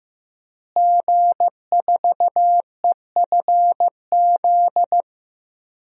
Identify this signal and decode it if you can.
Morse code, heard as a keyed tone, about 15 wpm, 700 Hz. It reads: G4EFZ
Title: DXCC-100 Call Signs - 5 Characters - Repeated 2x - 15wpm